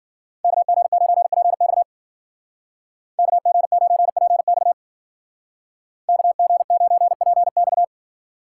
Morse code, heard as a keyed tone, about 49 wpm, 700 Hz. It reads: KG9PX KG9PX KG9PX
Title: Speed Racing - USA 2x2 Call Signs 40wpm